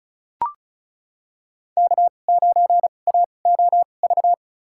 Morse code, heard as a keyed tone, about 36 wpm, spaced as for 25 wpm, 700 Hz. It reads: K9AOV
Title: USA Call Signs - 5 Characters - 2x Character Spacing - 35wpm